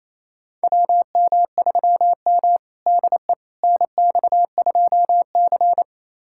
Morse code, heard as a keyed tone, about 28 wpm, 700 Hz. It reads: WM3M DE NX2C
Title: USA Callsign DE Callsign - 4 Characters - 28wpm